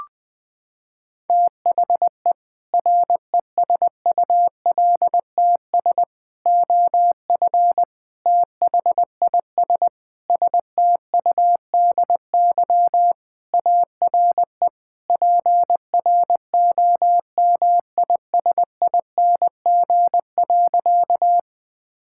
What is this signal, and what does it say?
Morse code, heard as a keyed tone, about 21 wpm, 700 Hz. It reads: THE RESULTS OF THIS STUDY ARE PROMISING.
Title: Sentences with 5-Word Phrases - 20wpm